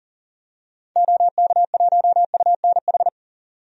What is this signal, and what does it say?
Morse code, heard as a keyed tone, about 40 wpm, 700 Hz. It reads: OK1UNH
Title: DXCC-100 Call Signs - 6 Characters - 40wpm